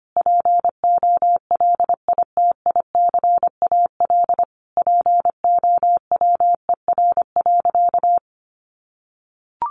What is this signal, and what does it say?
Morse code, heard as a keyed tone, about 25 wpm, 700 Hz. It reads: POLITICAL POWER.
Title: Sentences from Top 900 Words 25wpm